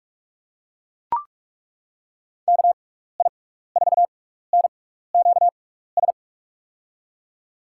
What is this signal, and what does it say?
Morse code, heard as a keyed tone, about 42 wpm, spaced as for 15 wpm, 700 Hz. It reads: KI4NQS
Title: USA Call Signs - 6 Characters - 6x Character Spacing - 45wpm